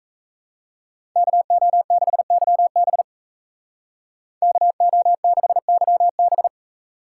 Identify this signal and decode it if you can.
Morse code, heard as a keyed tone, about 33 wpm, 700 Hz. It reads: KO6YB KO6YB
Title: Speed Racing - USA 2x2 Call Signs 28wpm